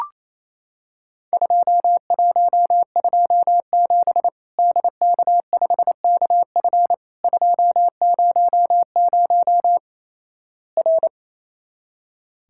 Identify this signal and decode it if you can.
Morse code, heard as a keyed tone, about 28 wpm, 700 Hz. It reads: 2127 DK5KF 200 R
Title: Worked All Europe DX Contest - QTC Exchange - 28wpm